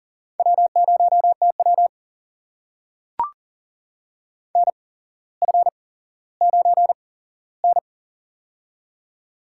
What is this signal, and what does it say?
Morse code, heard as a keyed tone, about 40 wpm, spaced as for 17 wpm, 700 Hz. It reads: W0TW NF9N
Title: USA Call Signs - 4 Characters - 8x Character Spacing - 40wpm